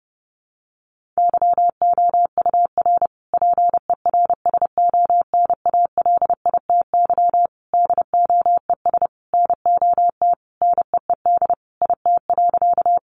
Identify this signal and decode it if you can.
Morse code, heard as a keyed tone, about 30 wpm, 700 Hz. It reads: YOUR PERSONALITY DOES NOT NEED IT.
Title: Sentences from Top 1400 Words 30wpm